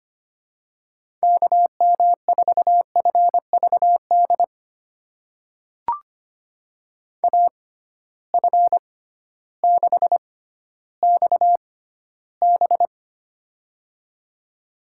Morse code, heard as a keyed tone, about 25 wpm, spaced as for 13 wpm, 700 Hz. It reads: KM4FVD AF6XB
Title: USA Call Signs - 6x Character Spacing - 25wpm